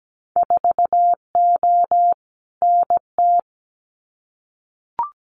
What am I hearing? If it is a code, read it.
Morse code, heard as a keyed tone, about 17 wpm, 700 Hz. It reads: 4O NT
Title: Winter Field Day Exchanges Encore 17wpm